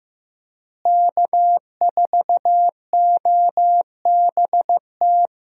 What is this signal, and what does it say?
Morse code, heard as a keyed tone, about 15 wpm, 700 Hz. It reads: K4OBT
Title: USA Call Signs - 5 Characters - Repeated 3x - 15wpm